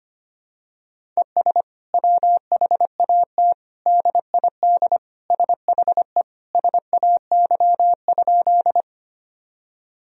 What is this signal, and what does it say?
Morse code, heard as a keyed tone, about 25 wpm, 700 Hz. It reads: ES WHAT DID SHE SAY?
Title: Sentences from Top 200 Words and Abbreviations 25wpm